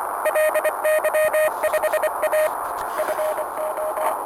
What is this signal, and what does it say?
Morse code, heard as a keyed tone, about 25 wpm, 620 Hz, with noise. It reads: LY5A FO